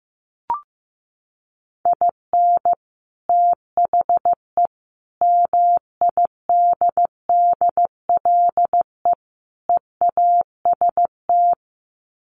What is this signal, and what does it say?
Morse code, heard as a keyed tone, about 15 wpm, 700 Hz. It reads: IN THE MIDDLE EAST
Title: Sets of 4 Words 15wpm